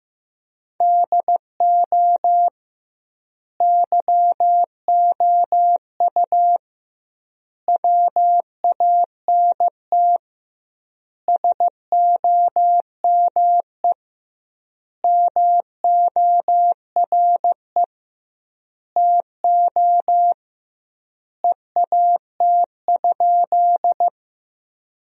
Morse code, heard as a keyed tone, about 15 wpm, 700 Hz. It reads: DO YOU WANT SOME MORE TO EAT?
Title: 2x Word-Spacing - Sentences from Top 300 Words - 15wpm